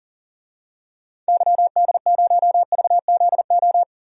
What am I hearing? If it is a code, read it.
Morse code, heard as a keyed tone, about 40 wpm, 700 Hz. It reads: YD0VZO
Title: DXCC-100 Call Signs - 6 Characters - 40wpm